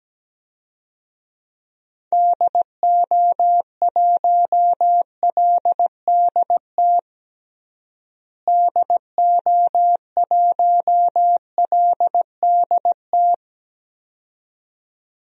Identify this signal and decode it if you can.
Morse code, heard as a keyed tone, about 17 wpm, 700 Hz. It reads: DO1LDT DO1LDT
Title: DXCC-100 Call Signs - 6 Characters - Repeated 2x - 17wpm